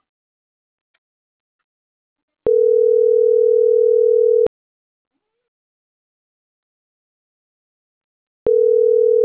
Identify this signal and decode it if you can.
Ringback tone (North America), 440 + 480 Hz.